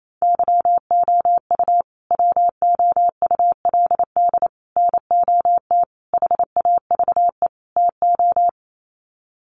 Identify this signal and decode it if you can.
Morse code, heard as a keyed tone, about 28 wpm, 700 Hz. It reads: YOU WOULD NOT HAVE TO
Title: Sets of 5 Words Encore 28wpm